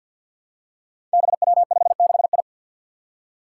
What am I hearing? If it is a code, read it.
Morse code, heard as a keyed tone, about 47 wpm, 700 Hz. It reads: BW56I
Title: Five Challenging Letter-Number Combinations 50wpm